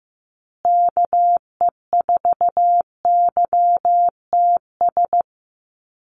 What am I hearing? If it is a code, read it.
Morse code, heard as a keyed tone, about 15 wpm, 700 Hz. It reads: KE4YTS
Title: USA Call Signs Repeated 3x 15wpm